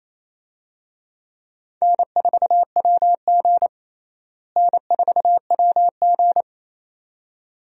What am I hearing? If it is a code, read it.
Morse code, heard as a keyed tone, about 28 wpm, 700 Hz. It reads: N4WG N4WG
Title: USA Call Signs - 4 Characters - Repeated 2x - 28wpm